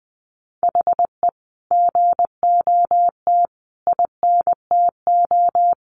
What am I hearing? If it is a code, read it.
Morse code, heard as a keyed tone, about 20 wpm, 700 Hz. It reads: HE GOT INTO
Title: Sentences from Top 1300 Words 20wpm